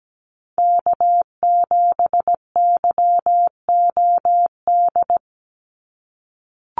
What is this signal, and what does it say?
Morse code, heard as a keyed tone, about 17 wpm, 700 Hz. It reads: K7YOD
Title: USA Call Signs 17wpm